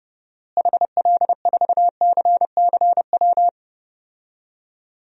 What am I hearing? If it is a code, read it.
Morse code, heard as a keyed tone, about 30 wpm, 700 Hz. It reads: HL4CCW
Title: DXCC-100 Call Signs - 6 Characters - Repeated 3x - 30wpm